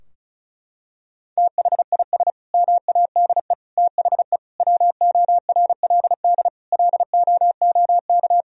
Morse code, heard as a keyed tone, about 36 wpm, 700 Hz. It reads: THIS MADE THE WORLD LOOK
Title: Sentences from Top 1000 Words - Review Part 2 - 35wpm